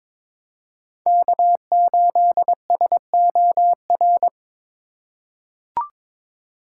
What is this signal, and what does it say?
Morse code, heard as a keyed tone, about 21 wpm, 700 Hz. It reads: K8SOR
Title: USA Call Signs - 5 Characters - 4x Character Spacing - 22wpm